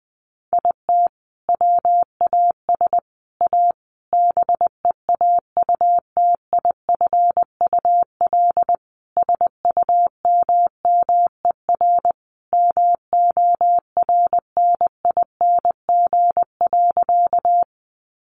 Morse code, heard as a keyed tone, about 20 wpm, 700 Hz. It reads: IT WAS A BEAUTIFUL SUMMER MORNING.
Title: Sentences from Top 1100 Words 20wpm